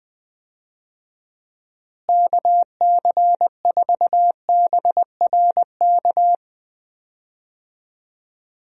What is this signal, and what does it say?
Morse code, heard as a keyed tone, about 19 wpm, 700 Hz. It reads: KC4BRK
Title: USA Call Signs - 6 Characters - Repeated 2x - 20wpm